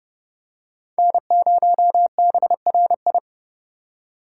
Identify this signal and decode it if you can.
Morse code, heard as a keyed tone, about 28 wpm, 700 Hz. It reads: N0BRI